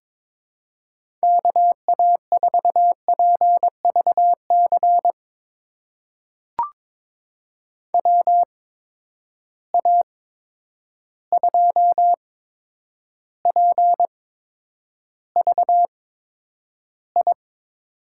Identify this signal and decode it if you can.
Morse code, heard as a keyed tone, about 22 wpm, spaced as for 9 wpm, 700 Hz. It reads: KA4PVC WA2PVI
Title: USA Call Signs - 6 Characters - 8x Character Spacing - 22wpm